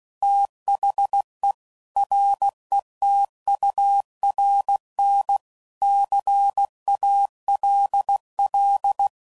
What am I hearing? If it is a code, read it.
Morse code, heard as a keyed tone, about 16 wpm, 800 Hz. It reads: THE RETURN CALL
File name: CW Code Groups @15WPM / QSOs / WA7QOX QRX